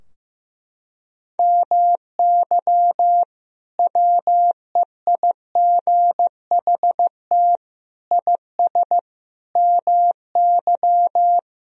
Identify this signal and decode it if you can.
Morse code, heard as a keyed tone, about 15 wpm, 700 Hz. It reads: MY WEIGHT IS MY
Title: Sentences from Top 1000 Words - Review Part 3 - 15wpm